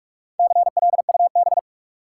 Morse code, heard as a keyed tone, about 45 wpm, 700 Hz. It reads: CLUB
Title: Sentences from Top 1200 Words - New Word Drill - 45wpm